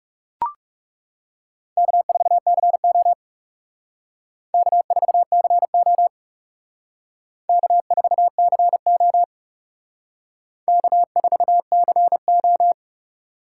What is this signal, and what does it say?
Morse code, heard as a keyed tone, about 38 wpm, 700 Hz. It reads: K4CO K4CO K4CO K4CO
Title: Speed Racing - USA Call Signs 30wpm